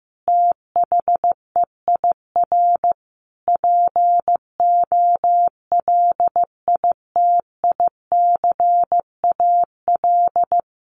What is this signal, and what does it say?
Morse code, heard as a keyed tone, about 15 wpm, 700 Hz. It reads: THEIR POLITICAL
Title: Sentences from Top 1400 Words 15wpm